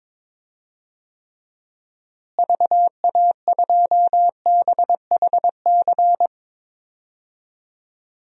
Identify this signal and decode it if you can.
Morse code, heard as a keyed tone, about 22 wpm, 700 Hz. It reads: VA2BHC
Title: DXCC-100 Call Signs - 6 Characters - Repeated 2x - 22wpm